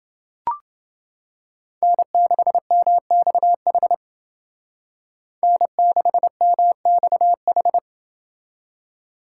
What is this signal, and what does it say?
Morse code, heard as a keyed tone, about 29 wpm, 700 Hz. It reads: N6MXH N6MXH